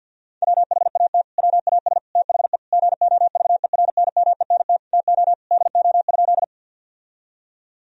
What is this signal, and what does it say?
Morse code, heard as a keyed tone, about 46 wpm, 700 Hz. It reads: WHAT WAS THE GOVERNMENT TO DO?
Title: Sentences from Top 1000 Words - Review Part 3 - 50wpm